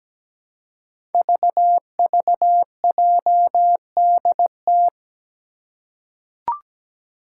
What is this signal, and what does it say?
Morse code, heard as a keyed tone, about 17 wpm, 700 Hz. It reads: VVJDT
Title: Five Challenging Letter-Number Combinations 17wpm